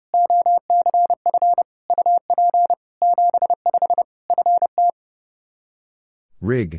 Morse code, heard as a keyed tone, about 30 wpm, 700 Hz. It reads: OCF UP 75 FT
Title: K1USN SST QSO Practice 30wpm